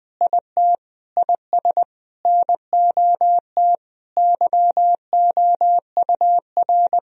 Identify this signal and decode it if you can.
Morse code, heard as a keyed tone, about 21 wpm, 700 Hz. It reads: IT IS NOT YOUR
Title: Sentences from Top 1000 Words - Review Part 1 - 20wpm